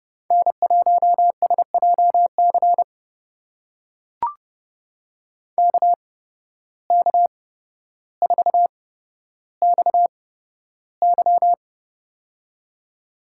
Morse code, heard as a keyed tone, about 29 wpm, spaced as for 13 wpm, 700 Hz. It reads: N1SJC KK4XY
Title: USA Call Signs - 5 Characters - 8x Character Spacing - 30wpm